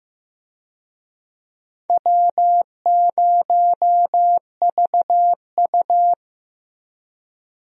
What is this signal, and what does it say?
Morse code, heard as a keyed tone, about 15 wpm, 700 Hz. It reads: W0VU
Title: USA Call Signs - 4 Characters - Repeated 2x - 15wpm